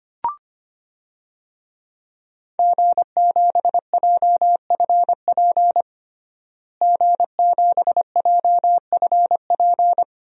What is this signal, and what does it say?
Morse code, heard as a keyed tone, about 25 wpm, 700 Hz. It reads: G7JFP G7JFP
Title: DXCC-100 Call Signs - 5 Characters - Repeated 2x - 25wpm